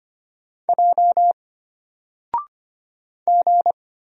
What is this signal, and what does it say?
Morse code, heard as a keyed tone, about 25 wpm, 700 Hz. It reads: J G
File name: Single-Letter-Number-25wpm